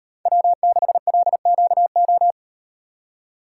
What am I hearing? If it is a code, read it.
Morse code, heard as a keyed tone, about 38 wpm, 700 Hz. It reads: W6LQO